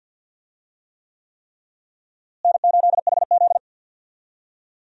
Morse code, heard as a keyed tone, about 51 wpm, 700 Hz. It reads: N8HZ
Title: USA Call Signs - 4 Characters - Repeated 3x - 50wpm